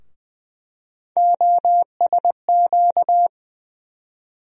Morse code, heard as a keyed tone, about 20 wpm, 700 Hz. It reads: OSQ
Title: USA Call Sign Postfixes - 3 Letters 20wpm